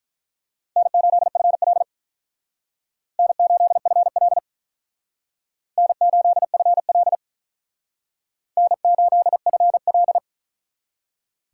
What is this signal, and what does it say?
Morse code, heard as a keyed tone, about 43 wpm, 700 Hz. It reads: N8FL N8FL N8FL N8FL
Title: Speed Racing - USA Call Signs 35wpm